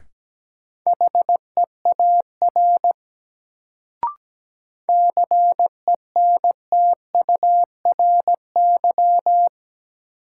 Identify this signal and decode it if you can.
Morse code, heard as a keyed tone, about 17 wpm, 700 Hz. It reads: HEAR CENTURY
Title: Sets of 5 Words - New Word Drill - 17wpm